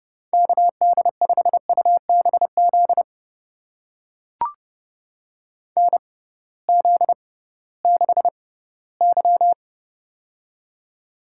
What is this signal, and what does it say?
Morse code, heard as a keyed tone, about 30 wpm, spaced as for 18 wpm, 700 Hz. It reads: KD5UBZ NZ6Y
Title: USA Call Signs - 6x Character Spacing - 30wpm